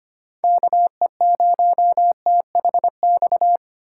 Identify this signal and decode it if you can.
Morse code, heard as a keyed tone, about 25 wpm, 700 Hz. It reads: KE0THX
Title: USA Call Signs - 3x Character Spacing - 25wpm